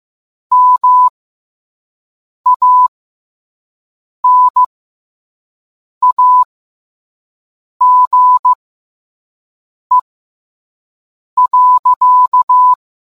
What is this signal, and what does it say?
Morse code, heard as a keyed tone, about 15 wpm, spaced as for 5 wpm, 1000 Hz. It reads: MANAGE.